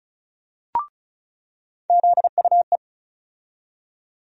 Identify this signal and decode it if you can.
Morse code, heard as a keyed tone, about 36 wpm, 700 Hz. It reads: ZUE
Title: USA Call Sign Postfixes - 3 Letters 35wpm